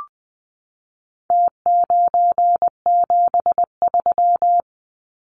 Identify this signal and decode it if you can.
Morse code, heard as a keyed tone, about 20 wpm, 700 Hz. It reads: T973
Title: Numbers - Contest Style - 4-Digit 20wpm